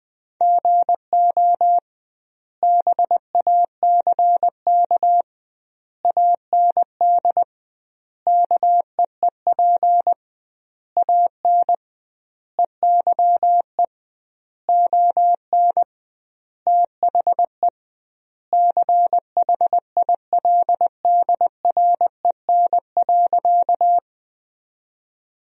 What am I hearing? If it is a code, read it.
Morse code, heard as a keyed tone, about 20 wpm, 700 Hz. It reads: GO BACK AND KEEP AN EYE ON THE CHILDREN.
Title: 2x Word-Spacing - Sentences from Top 300 Words - 20wpm